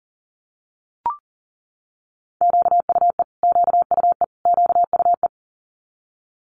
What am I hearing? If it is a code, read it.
Morse code, heard as a keyed tone, about 38 wpm, 700 Hz. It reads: QUE QUE QUE
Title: Multi-letter Phonemes Repeated 3x 40wpm